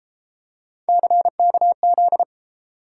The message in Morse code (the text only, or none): CKZ